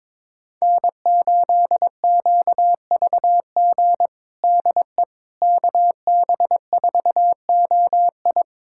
Morse code, heard as a keyed tone, about 22 wpm, 700 Hz. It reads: N8QVG DE KB4OI
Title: USA Callsign DE Callsign - 5 Characters - 22wpm